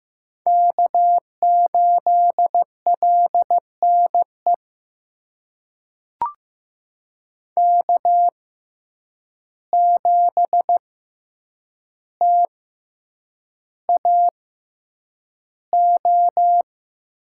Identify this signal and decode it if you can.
Morse code, heard as a keyed tone, about 15 wpm, spaced as for 8 wpm, 700 Hz. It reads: K8LNE K7TAO
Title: USA Call Signs - 5 Characters - 6x Character Spacing - 15wpm